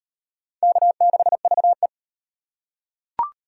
Morse code, heard as a keyed tone, about 37 wpm, 700 Hz. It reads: K6VE